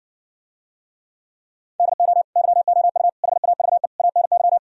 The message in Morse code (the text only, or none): DO YOU HAVE ANY